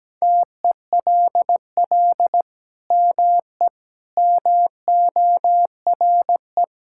Morse code, heard as a keyed tone, about 17 wpm, 700 Hz. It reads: TELL ME MORE
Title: Sentences from Top 1000 Words - Review Part 5 - 17wpm